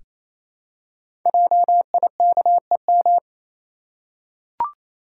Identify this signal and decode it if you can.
Morse code, heard as a keyed tone, about 28 wpm, 700 Hz. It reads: JIKEM
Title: Code Groups - 5 Letters - 28wpm